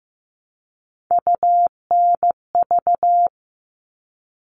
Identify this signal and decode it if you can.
Morse code, heard as a keyed tone, about 15 wpm, 700 Hz. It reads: UNV
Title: All Cut Numbers - 3-Digit 15wpm